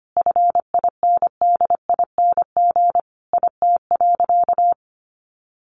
Morse code, heard as a keyed tone, about 25 wpm, 700 Hz. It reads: FINDING IT.
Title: Sentences from Top 1100 Words 25wpm